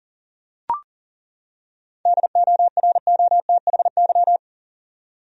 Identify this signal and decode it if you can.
Morse code, heard as a keyed tone, about 38 wpm, 700 Hz. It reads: DOROTHY
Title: USA Common Names - 40wpm